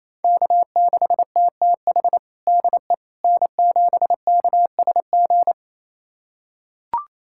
Morse code, heard as a keyed tone, about 28 wpm, 700 Hz. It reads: K6TTH DE N7KSG
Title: USA Callsign DE Callsign - 5 Characters - 28wpm